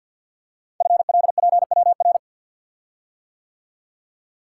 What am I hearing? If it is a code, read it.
Morse code, heard as a keyed tone, about 48 wpm, 700 Hz. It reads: FLPWR
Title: Code Groups - 5 Letters - 50wpm